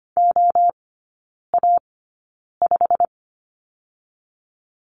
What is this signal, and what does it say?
Morse code, heard as a keyed tone, about 25 wpm, spaced as for 9 wpm, 700 Hz. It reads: OA5
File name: Lesson 15e - Number 5 - Three Characters 25wpm